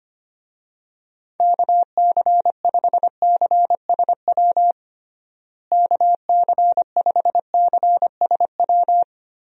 Morse code, heard as a keyed tone, about 25 wpm, 700 Hz. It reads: KC5CSW KC5CSW
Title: USA Call Signs - 6 Characters - Repeated 3x - 25wpm